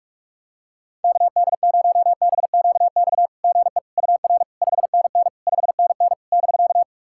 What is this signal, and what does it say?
Morse code, heard as a keyed tone, about 46 wpm, 690 Hz. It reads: KD0BQX GE UR 5NN 5NN <BK>
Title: SOTA QSO Practice 45wpm